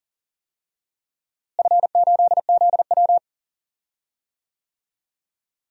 Morse code, heard as a keyed tone, about 40 wpm, 700 Hz. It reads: F8ZW
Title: DXCC-100 Call Signs - 4 Characters - Repeated 3x - 40wpm